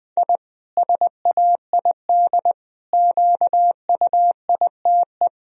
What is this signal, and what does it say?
Morse code, heard as a keyed tone, about 20 wpm, 700 Hz. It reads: I SAID QUITE
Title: Sentences from Top 1000 Words - Review Part 4 - 20wpm